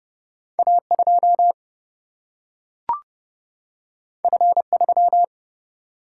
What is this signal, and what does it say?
Morse code, heard as a keyed tone, about 30 wpm, 700 Hz. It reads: A 2 F 3
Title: Single-Letter Single-Number 30wpm